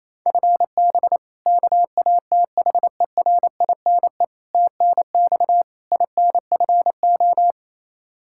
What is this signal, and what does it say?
Morse code, heard as a keyed tone, about 28 wpm, 700 Hz. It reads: FB KATHERINE TNX INFO
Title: Practice QSO - Exchange 3 - More Info - 28wpm